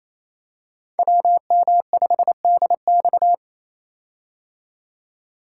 Morse code, heard as a keyed tone, about 28 wpm, 700 Hz. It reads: WM5DX
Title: K1USN SST QSO Practice - 28wpm